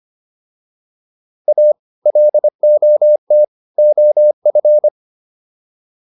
Morse code, heard as a keyed tone, about 25 wpm, 600 Hz. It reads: A LOT OF